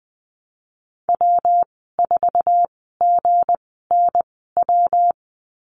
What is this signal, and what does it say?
Morse code, heard as a keyed tone, about 20 wpm, spaced as for 14 wpm, 700 Hz. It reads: W4GNW
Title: 179 Lesson 29g - Slash - Call Signs 20wpm